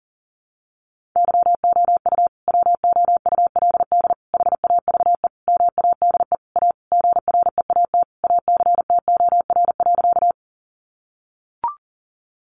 Morse code, heard as a keyed tone, about 40 wpm, 700 Hz. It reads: YOU WOULD HAVE MADE A GREAT ACTOR.
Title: Sentences from Top 1500 Words 40wpm